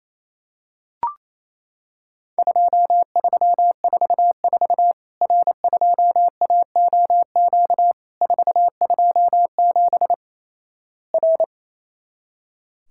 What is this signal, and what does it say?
Morse code, heard as a keyed tone, about 28 wpm, 700 Hz. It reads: 2344 R2AOQ 427 R
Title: Worked All Europe DX Contest - QTC Exchange - 28wpm